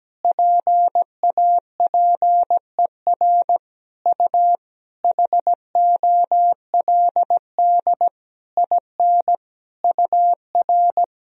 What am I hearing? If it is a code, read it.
Morse code, heard as a keyed tone, about 17 wpm, 700 Hz. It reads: PAPER U HOLD IN UR
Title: Sentences from Top 400 Words and Abbreviations 17wpm